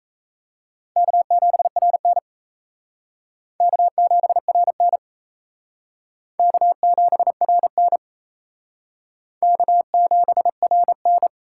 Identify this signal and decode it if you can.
Morse code, heard as a keyed tone, about 37 wpm, 700 Hz. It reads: K7RN K7RN K7RN K7RN